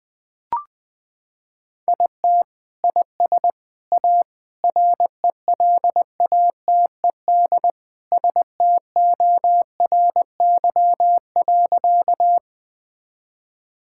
Morse code, heard as a keyed tone, about 20 wpm, 700 Hz. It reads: IT IS A RELATED STORY.
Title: Sentences from Top 1000 Words - Review Part 5 - 20wpm